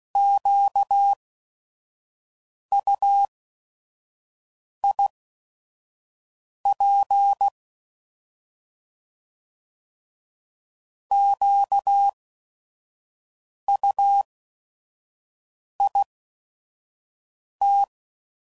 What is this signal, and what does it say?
Morse code, heard as a keyed tone, about 16 wpm, spaced as for 5 wpm, 800 Hz. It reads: QUIP QUIT
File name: Prefixes / By First Letter / Qs